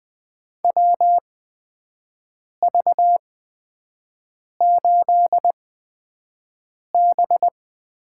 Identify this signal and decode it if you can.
Morse code, heard as a keyed tone, about 20 wpm, spaced as for 5 wpm, 700 Hz. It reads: WV8B
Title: USA Call Signs - 8x Character Spacing - 20wpm